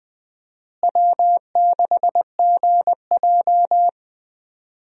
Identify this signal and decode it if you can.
Morse code, heard as a keyed tone, about 20 wpm, 700 Hz. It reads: W6GJ
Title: USA Call Signs - 4 Characters - Repeated 2x - 20wpm